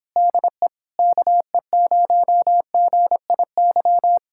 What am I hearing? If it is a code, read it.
Morse code, heard as a keyed tone, about 27 wpm, 700 Hz. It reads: DE KE0GIY